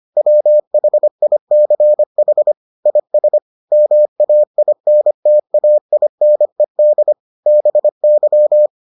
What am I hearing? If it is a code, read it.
Morse code, heard as a keyed tone, about 25 wpm, 600 Hz. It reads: WHICH IS MAINTAINED BY